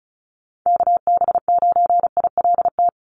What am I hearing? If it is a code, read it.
Morse code, heard as a keyed tone, about 34 wpm, 700 Hz. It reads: KB9ILT
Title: NA SOTA Activator Callsigns 35wpm